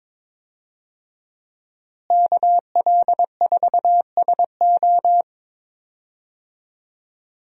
Morse code, heard as a keyed tone, about 22 wpm, 700 Hz. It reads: KL4SO
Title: USA Call Signs - 5 Characters - Repeated 3x - 22wpm